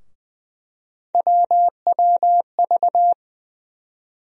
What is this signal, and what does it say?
Morse code, heard as a keyed tone, about 20 wpm, 700 Hz. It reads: WWV
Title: USA Call Sign Postfixes - 3 Letters 20wpm